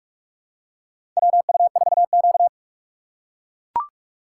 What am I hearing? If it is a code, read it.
Morse code, heard as a keyed tone, about 44 wpm, 690 Hz. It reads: WU4Q
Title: USA Call Signs - 4 Characters - 2x Character Spacing - 45wpm